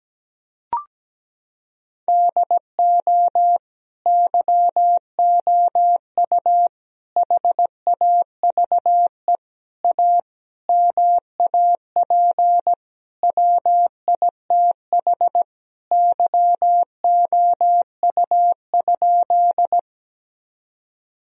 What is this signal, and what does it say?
Morse code, heard as a keyed tone, about 17 wpm, 700 Hz. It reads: DO YOU HAVE A MAP WITH YOU?
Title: Sentences from Top 1000 Words - Review Part 2 - 17wpm